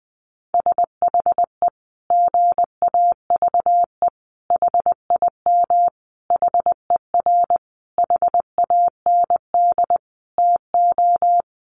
Morse code, heard as a keyed tone, about 20 wpm, 700 Hz. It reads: SHE GAVE HIM HER HAND TO
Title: Sentences from Top 1400 Words 20wpm